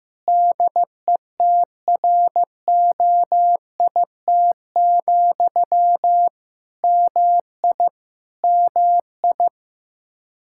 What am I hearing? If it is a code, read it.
Morse code, heard as a keyed tone, about 15 wpm, 700 Hz. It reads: DETROIT, MI MI
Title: Practice QSO - Exchange 1 - Beginning - 15wpm